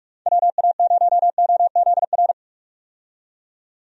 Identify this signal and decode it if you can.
Morse code, heard as a keyed tone, about 46 wpm, 700 Hz. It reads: WA0OZR
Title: USA Call Signs - 6 Characters - 45wpm